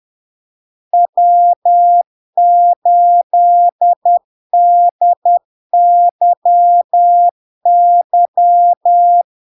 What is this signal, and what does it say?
Morse code, heard as a keyed tone, about 10 wpm, 700 Hz. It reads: W8DYY
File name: LARK-FD-2022-Log-in-CW-10wpm-700-Hz